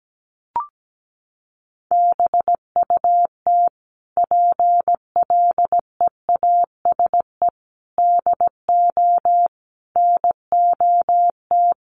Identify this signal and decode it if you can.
Morse code, heard as a keyed tone, about 17 wpm, 700 Hz. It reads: BUT PLEASE DO NOT
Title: Sentences from Top 1500 Words 17wpm